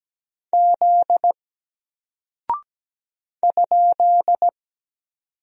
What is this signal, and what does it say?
Morse code, heard as a keyed tone, about 17 wpm, 700 Hz. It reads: Z ?
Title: Single Character (Letter:Number:Punctuation:Prosign) Alt 17wpm